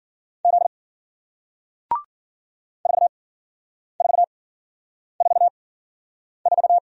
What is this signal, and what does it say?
Morse code, heard as a keyed tone, about 54 wpm, spaced as for 9 wpm, 700 Hz. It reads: Z 4444